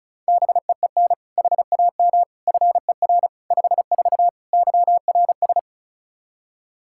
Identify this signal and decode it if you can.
Morse code, heard as a keyed tone, about 34 wpm, 700 Hz. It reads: BEEN HAM FER 54 YRS
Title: K1USN SST QSO Practice 35wpm